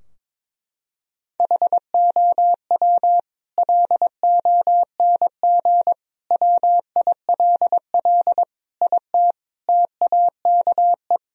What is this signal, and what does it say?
Morse code, heard as a keyed tone, about 22 wpm, 700 Hz. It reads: HOW LONG WILL IT TAKE
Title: Sentences from Top 1000 Words - Review Part 2 - 22wpm